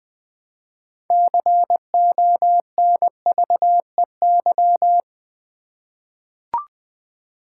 Morse code, heard as a keyed tone, about 20 wpm, 700 Hz. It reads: CONVEY
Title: 6 Letter Words - First 3 Letters Given - 20wpm